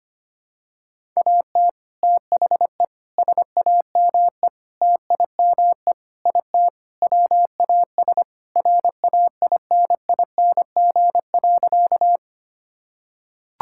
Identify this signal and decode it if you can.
Morse code, heard as a keyed tone, about 25 wpm, 700 Hz. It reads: AT THE SAME TIME IT WAS RAINING.
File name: practice / 25wpm / Sentences with 3-Word Phrases - 25wpm